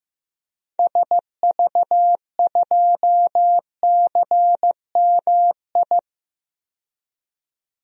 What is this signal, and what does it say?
Morse code, heard as a keyed tone, about 15 wpm, 700 Hz. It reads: SV2CMI
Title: DXCC-100 Call Signs - 6 Characters - 15wpm